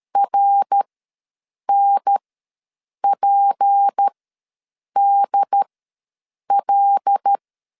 Morse code, heard as a keyed tone, about 13 wpm, spaced as for 7 wpm, 800 Hz. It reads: RNPDL
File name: LEZIONI / 6 PKX0 / 6 P ws 12wpm 3117